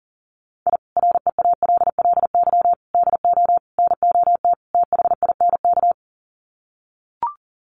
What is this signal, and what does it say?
Morse code, heard as a keyed tone, about 40 wpm, 700 Hz. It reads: I REALLY DO NOT THINK
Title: Sets of 5 Words Encore2 40wpm